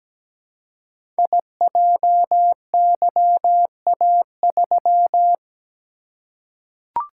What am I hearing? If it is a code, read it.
Morse code, heard as a keyed tone, about 17 wpm, 700 Hz. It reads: IJYA3
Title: Code Groups - 5 Letter:Numbers - 17wpm